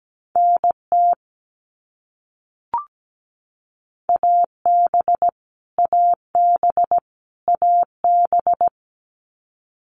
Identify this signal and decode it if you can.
Morse code, heard as a keyed tone, about 17 wpm, 700 Hz. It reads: NT AB AB AB
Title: Canadian Provinces and Territories - Repeated 3x - 17wpm